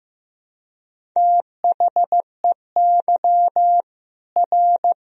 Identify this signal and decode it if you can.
Morse code, heard as a keyed tone, about 16 wpm, 700 Hz. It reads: THEY R
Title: Sentences from Top 300 Words and Abbreviations 15wpm